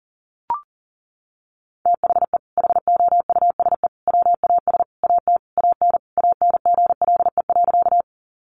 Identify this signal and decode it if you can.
Morse code, heard as a keyed tone, about 40 wpm, 700 Hz. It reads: THE HOUSE WAS AT AN ANGLE.
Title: Sentences from Top 1500 Words 40wpm